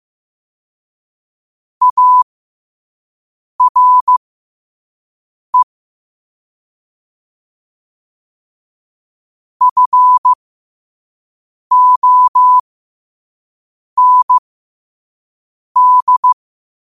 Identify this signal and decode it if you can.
Morse code, heard as a keyed tone, about 15 wpm, spaced as for 5 wpm, 1000 Hz. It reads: ARE FOND
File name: qotdfiles / 5wpm